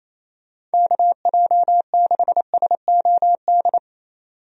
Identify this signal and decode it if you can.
Morse code, heard as a keyed tone, about 28 wpm, 700 Hz. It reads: KJ6SOD
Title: USA Call Signs - 6 Characters - 28wpm